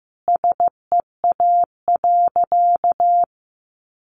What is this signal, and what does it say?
Morse code, heard as a keyed tone, about 15 wpm, 700 Hz. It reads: SEA.
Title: Sentences from Top 1300 Words 15wpm